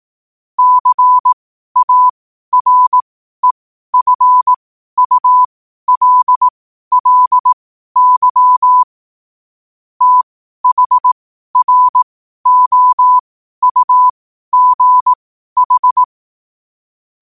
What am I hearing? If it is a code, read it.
Morse code, heard as a keyed tone, about 18 wpm, spaced as for 12 wpm, 1000 Hz. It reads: CAREFULLY THROUGH
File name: qotdfiles / 12wpm